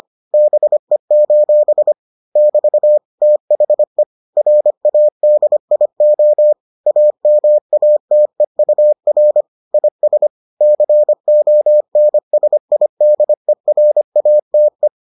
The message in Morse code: BE: <BT> THE RADIO AMATEUR IS CONSIDERATE

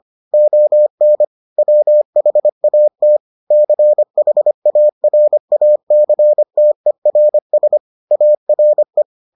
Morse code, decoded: ON WHAT CHARACTERS ARE

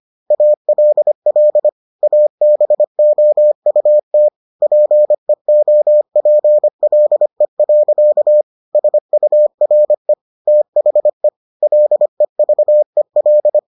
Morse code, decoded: ALL ABOUT PEOPLE. SURE THE LEVEL

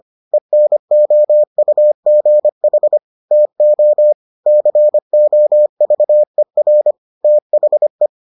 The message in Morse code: ENOUGH TO COVER THE

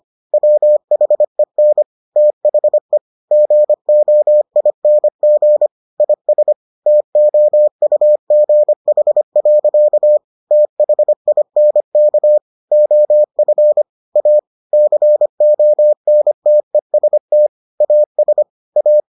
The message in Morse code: WHEN THE GOING IS TOUGH. THINK OF A CONTEST AS A

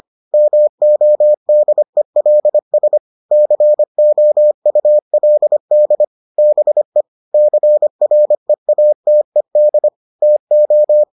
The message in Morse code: MODELS COULD BE CREATED TO

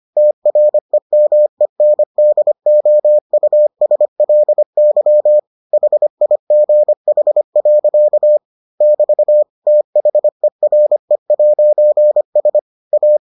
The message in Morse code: TREMENDOUSLY HIGH. <BT> THERE'S A

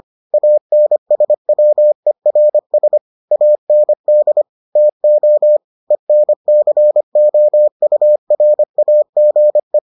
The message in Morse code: ANSWERS AND TO ENCOURAGE

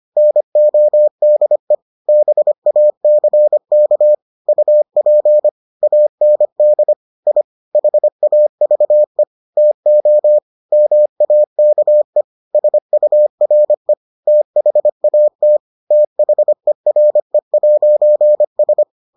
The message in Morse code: NODE BACK UP AND I HAVE TO MAKE SURE THAT THERE'S

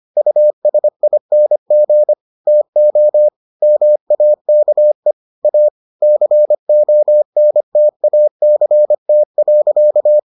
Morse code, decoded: USING TO MAKE A CONTACT.

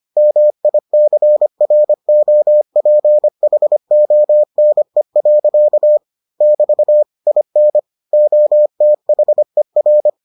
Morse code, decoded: MICROPHONE. <BT> IN OTHER

